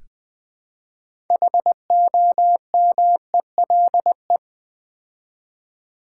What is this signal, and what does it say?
Morse code, heard as a keyed tone, about 20 wpm, 700 Hz. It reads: HOMELE
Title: Build A Word - 3 Letters At A Time - 8-Letter Words - 20wpm